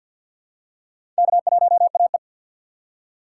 Morse code, heard as a keyed tone, about 47 wpm, 700 Hz. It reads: K1AE
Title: USA Call Signs - 4 Characters - 2x Character Spacing - 50wpm